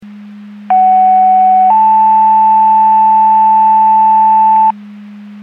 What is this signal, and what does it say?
Two-tone page, heard as a one-second tone, then a three-second tone, about 750 then 880 Hz. It is heard from 0.7 to 4.7 s.
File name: Portage / 03042 08-33 Portage 154.385 Paging